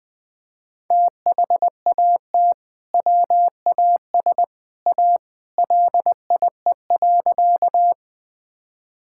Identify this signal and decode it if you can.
Morse code, heard as a keyed tone, about 20 wpm, 700 Hz. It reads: THAT WAS A LIE.
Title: Sentences from Top 1600 Words 20wpm